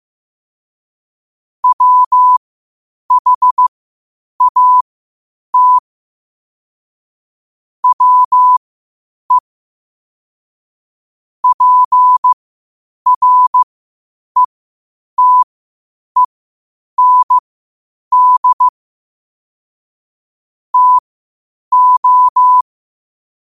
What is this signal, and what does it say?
Morse code, heard as a keyed tone, about 15 wpm, spaced as for 8 wpm, 1000 Hz. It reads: WHAT WE PRETEND TO
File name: qotdfiles / 8wpm